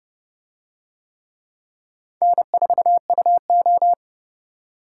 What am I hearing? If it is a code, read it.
Morse code, heard as a keyed tone, about 27 wpm, 700 Hz. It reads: N4UO